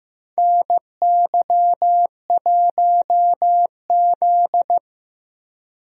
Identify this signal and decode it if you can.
Morse code, heard as a keyed tone, about 16 wpm, 700 Hz. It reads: NY1Z
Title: USA Call Signs - 4 Characters - Repeated 3x - 15wpm